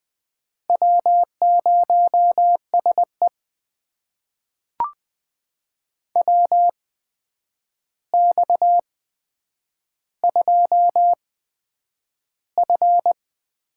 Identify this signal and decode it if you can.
Morse code, heard as a keyed tone, about 21 wpm, spaced as for 9 wpm, 700 Hz. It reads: W0SE WX2F
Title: USA Call Signs - 4 Characters - 8x Character Spacing - 20wpm